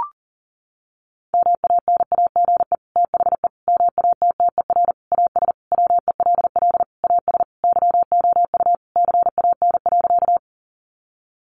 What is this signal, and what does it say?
Morse code, heard as a keyed tone, about 40 wpm, 700 Hz. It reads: MANAGE THE MATTER AS WELL AS YOU CAN.
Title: Sentences from Top 1400 Words 40wpm